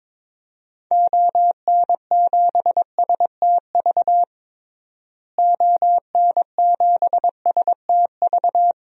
Morse code, heard as a keyed tone, about 22 wpm, 700 Hz. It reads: ON7STV ON7STV
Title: DXCC-100 Call Signs - 6 Characters - Repeated 2x - 22wpm